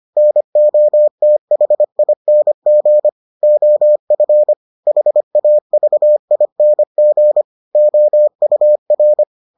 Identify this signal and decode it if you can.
Morse code, heard as a keyed tone, about 25 wpm, 600 Hz. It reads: NOTHING OF HAVING OUR